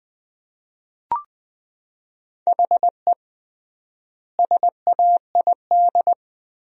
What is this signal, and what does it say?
Morse code, heard as a keyed tone, about 20 wpm, 700 Hz. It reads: HE SAID
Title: 3x Word-Spacing - Sentences from Top 200 Words - 20wpm